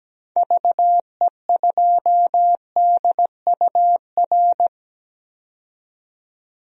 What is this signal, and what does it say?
Morse code, heard as a keyed tone, about 17 wpm, 700 Hz. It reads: VE2DUR
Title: DXCC-100 Call Signs - 6 Characters - 17wpm